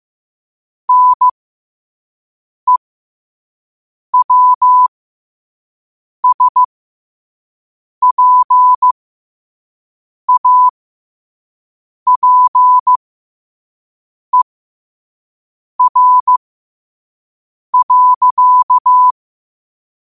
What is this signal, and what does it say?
Morse code, heard as a keyed tone, about 15 wpm, spaced as for 5 wpm, 1000 Hz. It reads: NEWSPAPER.
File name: qotdfiles / 5wpm